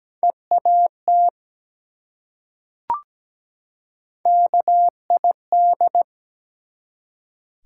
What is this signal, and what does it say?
Morse code, heard as a keyed tone, about 17 wpm, 700 Hz. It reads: EAT KID
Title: Three Letter Words - 17wpm